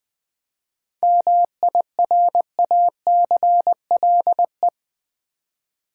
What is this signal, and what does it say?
Morse code, heard as a keyed tone, about 20 wpm, 700 Hz. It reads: MIRACLE
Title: 7 Letter Words - First 3 Letters Given - 20wpm